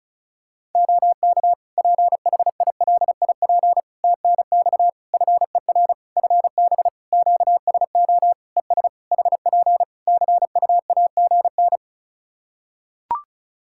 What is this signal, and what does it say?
Morse code, heard as a keyed tone, about 36 wpm, 700 Hz. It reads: OK PHILIP TNX FER FB QSO ES HP CUAGN
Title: K1USN SST QSO Practice 35wpm